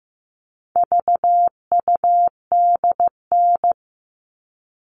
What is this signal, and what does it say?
Morse code, heard as a keyed tone, about 15 wpm, 700 Hz. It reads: VUDN
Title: All Cut Numbers - 4-Digit 15wpm